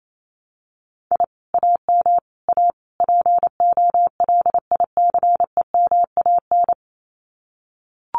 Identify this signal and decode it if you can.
Morse code, heard as a keyed tone, about 28 wpm, 700 Hz. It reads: I AM A POLICEMAN
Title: Common QSO Phrases Encore 28wpm